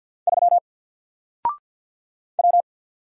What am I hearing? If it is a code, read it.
Morse code, heard as a keyed tone, about 50 wpm, 700 Hz. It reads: 3 W
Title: Single Character (Letter:Number:Punctuation:Prosign) 50wpm